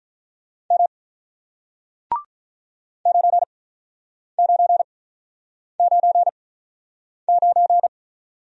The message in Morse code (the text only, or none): M 9 9 9 9